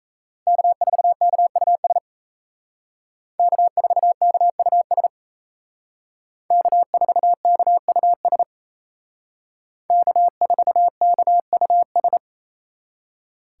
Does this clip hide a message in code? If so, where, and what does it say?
Yes: Morse code from 0.5 to 12.2 s, heard as a keyed tone, about 36 wpm, 700 Hz. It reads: K4KUS K4KUS K4KUS K4KUS